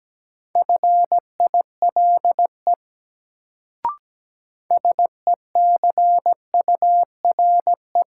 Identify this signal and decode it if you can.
Morse code, heard as a keyed tone, about 17 wpm, 700 Hz. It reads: FILE SECURE
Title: Sentences from Top 1200 Words - New Word Drill Alt - 17wpm